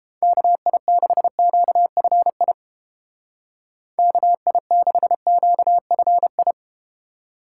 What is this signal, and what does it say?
Morse code, heard as a keyed tone, about 33 wpm, 700 Hz. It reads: KI6QFI KI6QFI